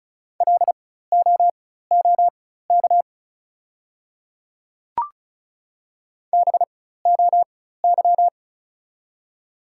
Morse code, heard as a keyed tone, about 36 wpm, spaced as for 16 wpm, 700 Hz. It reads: LOOK BOY
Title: 4x Character Spacing - Top 200 Words - 35wpm